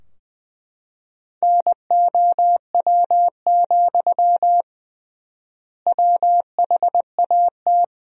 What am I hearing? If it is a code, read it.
Morse code, heard as a keyed tone, about 20 wpm, 700 Hz. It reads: NOW, WHAT
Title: 3x Word-Spacing - Sentences from Top 200 Words - 20wpm